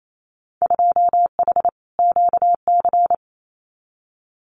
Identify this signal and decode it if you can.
Morse code, heard as a keyed tone, about 28 wpm, 700 Hz. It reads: 2H QC
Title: Winter Field Day Exchanges 28wpm